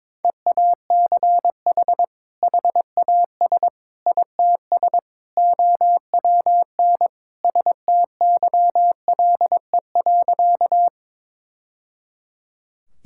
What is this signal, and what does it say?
Morse code, heard as a keyed tone, about 21 wpm, 700 Hz. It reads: EACH HAS ITS OWN STYLE.
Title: Sentences from Top 1000 Words - Review Part 5 - 22wpm